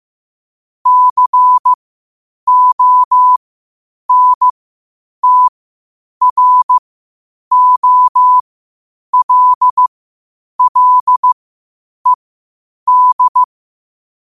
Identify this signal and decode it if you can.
Morse code, heard as a keyed tone, about 15 wpm, spaced as for 8 wpm, 1000 Hz. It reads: CONTROLLED
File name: qotdfiles / 8wpm